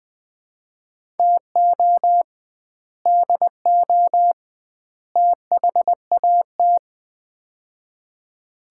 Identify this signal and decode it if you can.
Morse code, heard as a keyed tone, about 20 wpm, 700 Hz. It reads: TO DO THAT
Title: 2x Word-Spacing - Sets of 3 Words - 20wpm